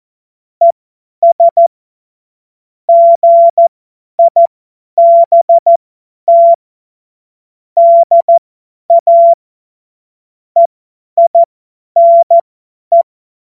Morse code, heard as a keyed tone, about 14 wpm, spaced as for 10 wpm, 680 Hz. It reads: ES GIBT DA EINE
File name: Morsekurs / Witze / Witz 19 März 2024 50 F70